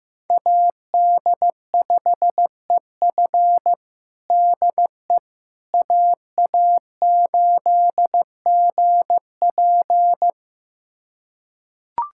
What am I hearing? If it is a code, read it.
Morse code, heard as a keyed tone, about 16 wpm, 700 Hz. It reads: AD5EF DE AA8GP
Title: USA Callsign DE Callsign - 5 Characters - 15wpm